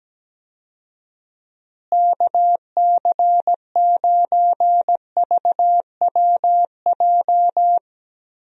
Morse code, heard as a keyed tone, about 17 wpm, 700 Hz. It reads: KC9VWJ
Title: USA Call Signs - 6 Characters - Repeated 3x - 17wpm